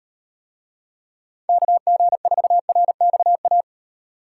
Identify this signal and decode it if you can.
Morse code, heard as a keyed tone, about 36 wpm, 700 Hz. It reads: KG4RXA